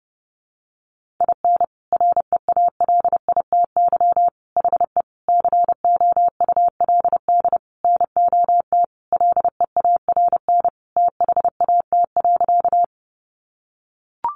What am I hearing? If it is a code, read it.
Morse code, heard as a keyed tone, about 30 wpm, 700 Hz. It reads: IN REALITY HE COULD NOT LEARN THAT.
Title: Sentences from Top 1200 Words 30wpm